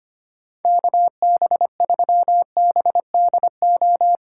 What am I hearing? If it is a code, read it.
Morse code, heard as a keyed tone, about 25 wpm, 700 Hz. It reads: KB3BDO
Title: USA Callsign DE Callsign 25wpm